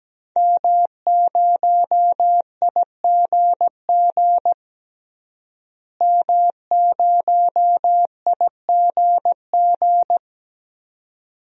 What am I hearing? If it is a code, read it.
Morse code, heard as a keyed tone, about 17 wpm, 700 Hz. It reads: M0IGG M0IGG
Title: DXCC-100 Call Signs - 5 Characters - Repeated 2x - 17wpm